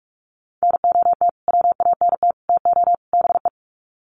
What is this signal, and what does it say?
Morse code, heard as a keyed tone, about 44 wpm, 700 Hz. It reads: NOT WANT TO BE
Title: Sets of 4 Words 45wpm